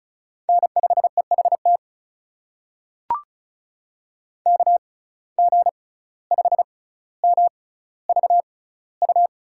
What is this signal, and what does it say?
Morse code, heard as a keyed tone, about 36 wpm, spaced as for 17 wpm, 700 Hz. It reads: N5EHT KG5MVU
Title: USA Call Signs - 6x Character Spacing - 35wpm